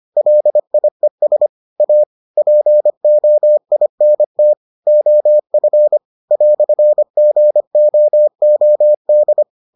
Morse code, heard as a keyed tone, about 25 wpm, 600 Hz. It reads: LIES A POINT OF "GOOD